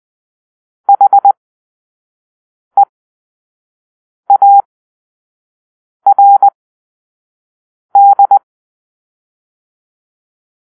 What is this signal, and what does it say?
Morse code, heard as a keyed tone, about 20 wpm, spaced as for 5 wpm, 800 Hz. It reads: HEARD
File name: arrl / Grp-2 Drill-4